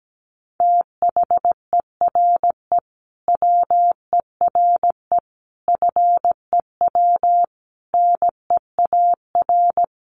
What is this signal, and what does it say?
Morse code, heard as a keyed tone, about 17 wpm, 700 Hz. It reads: THERE WERE FEW NEAR
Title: Sentences from Top 300 Words 17wpm